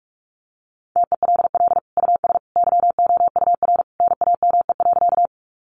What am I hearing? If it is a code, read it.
Morse code, heard as a keyed tone, about 44 wpm, 700 Hz. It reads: TELL US YOUR NAME.
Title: Sentences from Top 200 Words 45wpm